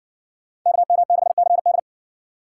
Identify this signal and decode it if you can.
Morse code, heard as a keyed tone, about 57 wpm, 700 Hz. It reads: KM6CD